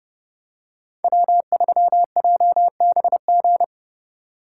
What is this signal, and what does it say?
Morse code, heard as a keyed tone, about 32 wpm, 700 Hz. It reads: W3JBG